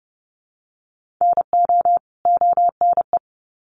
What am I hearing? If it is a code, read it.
Morse code, heard as a keyed tone, about 30 wpm, 700 Hz. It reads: NO ONE
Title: Sets of 2 Words 30wpm